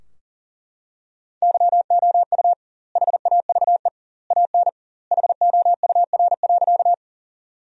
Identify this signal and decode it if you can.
Morse code, heard as a keyed tone, about 40 wpm, 700 Hz. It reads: YOU HAVE AN HOUR.
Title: 2x Word-Spacing - Sentences from Top 400 Words - 40wpm